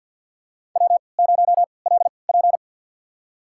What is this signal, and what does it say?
Morse code, heard as a keyed tone, about 50 wpm, spaced as for 28 wpm, 700 Hz. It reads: W0LP
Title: USA Call Signs - 4 Characters - 3x Character Spacing - 50wpm